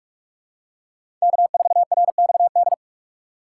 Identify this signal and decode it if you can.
Morse code, heard as a keyed tone, about 42 wpm, 700 Hz. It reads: K4RXD
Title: USA Call Signs - 3x Character Spacing - 45wpm